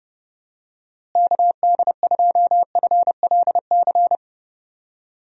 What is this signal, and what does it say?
Morse code, heard as a keyed tone, about 30 wpm, 700 Hz. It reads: KD2FLC
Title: USA Call Signs - 6 Characters - Repeated 2x - 30wpm